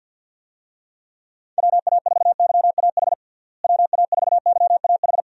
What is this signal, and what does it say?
Morse code, heard as a keyed tone, about 52 wpm, 700 Hz. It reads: WA4YAH WA4YAH
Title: USA Call Signs - 6 Characters - Repeated 2x - 50wpm